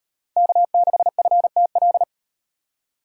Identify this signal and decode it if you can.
Morse code, heard as a keyed tone, about 36 wpm, 700 Hz. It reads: K6FTL